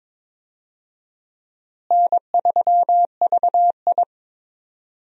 Morse code, heard as a keyed tone, about 22 wpm, 700 Hz. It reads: N3VI
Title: USA Call Signs - 4 Characters - Repeated 2x - 22wpm